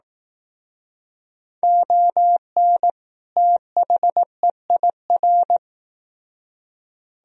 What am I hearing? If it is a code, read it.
Morse code, heard as a keyed tone, about 18 wpm, 700 Hz. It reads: ON THEIR
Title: Speed-Racing - Sets of 2 Words 15wpm